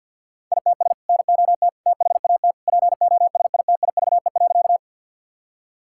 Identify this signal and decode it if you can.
Morse code, heard as a keyed tone, about 50 wpm, 700 Hz. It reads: ITS NOT THAT POSITIVE.
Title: Sentences from Top 1000 Words - Review Part 5 - 50wpm